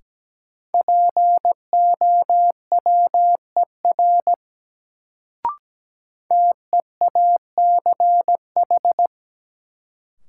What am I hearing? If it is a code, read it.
Morse code, heard as a keyed tone, about 17 wpm, 700 Hz. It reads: POWER TEACH
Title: Sentences from Top 500 Words - New Word Drill - 17wpm